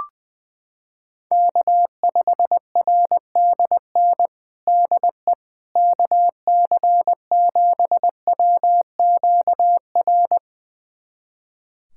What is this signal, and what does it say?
Morse code, heard as a keyed tone, about 21 wpm, 700 Hz. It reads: K5RDN DE KC7WQR
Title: USA Callsign DE Callsign 20wpm